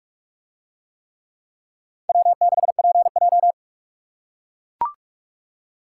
Morse code, heard as a keyed tone, about 44 wpm, 700 Hz. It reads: W6PJ